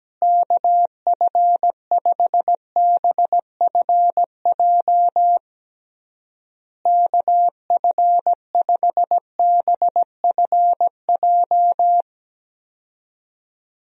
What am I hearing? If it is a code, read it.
Morse code, heard as a keyed tone, about 17 wpm, 700 Hz. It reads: KF5BFJ KF5BFJ
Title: USA Call Signs - 6 Characters - Repeated 3x - 17wpm